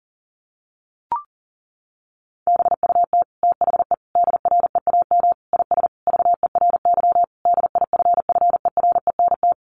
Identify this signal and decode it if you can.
Morse code, heard as a keyed tone, about 40 wpm, 700 Hz. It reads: BUT THE DREAM IS VERY DIFFERENT
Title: Sentences from Top 1500 Words 40wpm